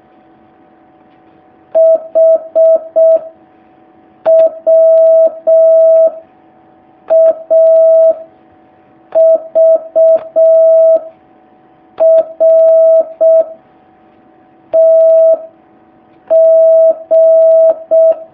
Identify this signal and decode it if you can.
Morse code, heard as a keyed tone, about 6 wpm, 650 Hz. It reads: H W A V R T G
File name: jh8cbh / 2019cw